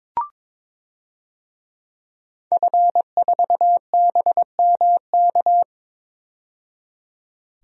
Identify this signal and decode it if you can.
Morse code, heard as a keyed tone, about 22 wpm, 700 Hz. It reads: F4BMK
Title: DXCC-100 Call Signs - 5 Characters - 22wpm